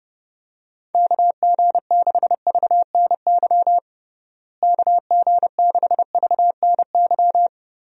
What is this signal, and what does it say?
Morse code, heard as a keyed tone, about 32 wpm, 700 Hz. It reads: KG6VNY KG6VNY
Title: USA Call Signs - 6 Characters - Repeated 3x - 30wpm